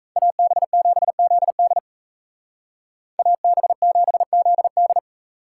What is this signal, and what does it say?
Morse code, heard as a keyed tone, about 36 wpm, 700 Hz. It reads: AB7ZD AB7ZD